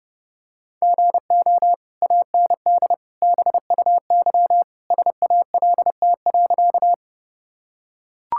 Morse code, heard as a keyed tone, about 30 wpm, 700 Hz. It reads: GO AND BUY SALT.
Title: Sentences from Top 1000 Words - Review Part 5 - 30wpm